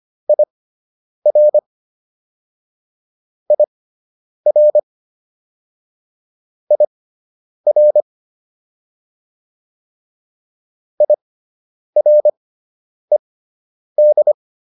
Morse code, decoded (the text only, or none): IR IR IR IRED